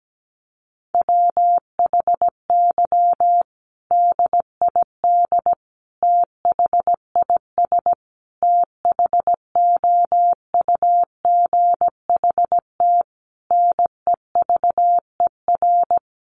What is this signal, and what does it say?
Morse code, heard as a keyed tone, about 17 wpm, 700 Hz. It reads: WHY DID THIS THOUGHT NEVER
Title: Sentences from Top 900 Words 17wpm